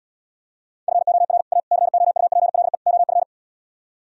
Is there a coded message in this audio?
Yes: Morse code from 0.9 to 3.2 s, heard as a keyed tone, about 77 wpm, 700 Hz. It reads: FOR A COUPLE OF